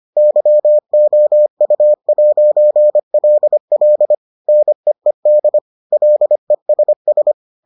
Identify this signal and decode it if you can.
Morse code, heard as a keyed tone, about 25 wpm, 600 Hz. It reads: YOU'LL NEED LESS